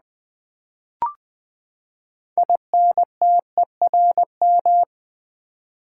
Morse code, heard as a keyed tone, about 20 wpm, 700 Hz. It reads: INTERM
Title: Build A Word - 6 Letters At A Time - 12-Letter Words - 20wpm